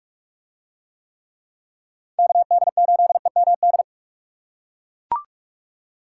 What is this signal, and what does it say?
Morse code, heard as a keyed tone, about 44 wpm, 700 Hz. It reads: KD8EMD